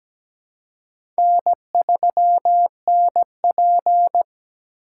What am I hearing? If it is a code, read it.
Morse code, heard as a keyed tone, about 17 wpm, 700 Hz. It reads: N3NP
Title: USA Call Signs - 4 Characters - 2x Character Spacing - 17wpm